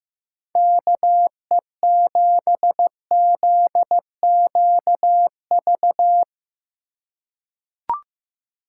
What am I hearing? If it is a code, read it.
Morse code, heard as a keyed tone, about 15 wpm, 700 Hz. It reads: KE7ZQV
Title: USA Call Signs - 6 Characters - 2x Character Spacing - 15wpm